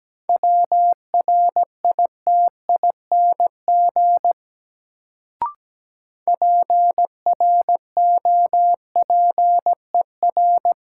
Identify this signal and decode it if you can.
Morse code, heard as a keyed tone, about 17 wpm, 700 Hz. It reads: WRITING PROPER
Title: Sentences from Top 800 Words - New Word Drill - 17wpm